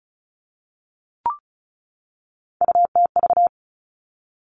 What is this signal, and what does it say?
Morse code, heard as a keyed tone, about 36 wpm, 700 Hz. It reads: UTV
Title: Three Challenging Letter-Number Combinations 35wpm